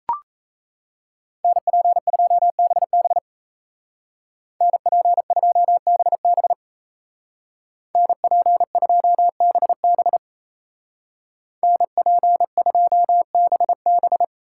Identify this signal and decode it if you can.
Morse code, heard as a keyed tone, about 37 wpm, 700 Hz. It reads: NP2BB NP2BB NP2BB NP2BB